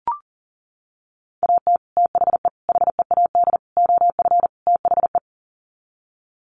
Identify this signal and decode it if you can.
Morse code, heard as a keyed tone, about 40 wpm, 700 Hz. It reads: AT THE HEAD OF THE
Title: Sets of 5 Words 40wpm